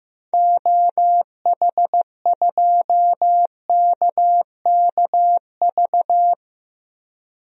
Morse code, heard as a keyed tone, about 15 wpm, 700 Hz. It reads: OH2KKV
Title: DXCC-100 Call Signs - 6 Characters - Repeated 3x - 15wpm